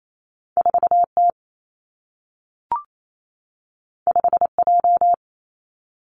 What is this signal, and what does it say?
Morse code, heard as a keyed tone, about 28 wpm, 700 Hz. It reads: 4 T 5 J
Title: Two Challenging Letter-Number Combinations 28wpm